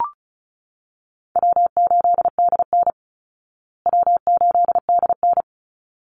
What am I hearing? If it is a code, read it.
Morse code, heard as a keyed tone, about 34 wpm, 700 Hz. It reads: W8DN W8DN
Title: Easy Call Signs Repeated 2x - 35wpm